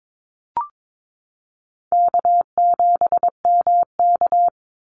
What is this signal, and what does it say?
Morse code, heard as a keyed tone, about 22 wpm, 700 Hz. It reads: K7MK
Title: NA SOTA Activator Callsigns 22wpm